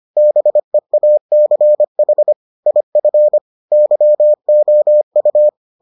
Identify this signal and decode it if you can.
Morse code, heard as a keyed tone, about 25 wpm, 600 Hz. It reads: BEACH IF YOU